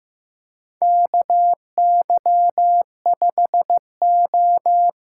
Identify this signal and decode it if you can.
Morse code, heard as a keyed tone, about 15 wpm, 700 Hz. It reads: KY5O
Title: USA Call Signs - 4 Characters - 4x Character Spacing - 15wpm